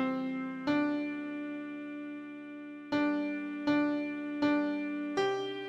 There is no code.